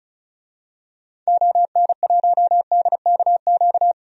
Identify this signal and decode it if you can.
Morse code, heard as a keyed tone, about 34 wpm, 700 Hz. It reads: ON1DKQ
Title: DXCC-100 Call Signs - 6 Characters - 35wpm